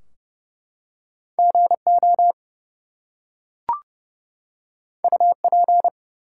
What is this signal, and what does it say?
Morse code, heard as a keyed tone, about 31 wpm, 700 Hz. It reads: GO UP
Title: Two Letter Words - 30wpm